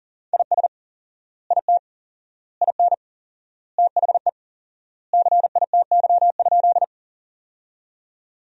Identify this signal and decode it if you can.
Morse code, heard as a keyed tone, about 40 wpm, 700 Hz. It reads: IS IT IN THE CITY?
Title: 4x Word-Spacing - Sentences from Top 300 Words - 40wpm